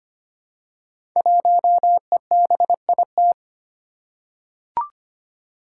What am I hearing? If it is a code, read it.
Morse code, heard as a keyed tone, about 25 wpm, 700 Hz. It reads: 1EBIT
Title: Five Challenging Letter-Number Combinations 25wpm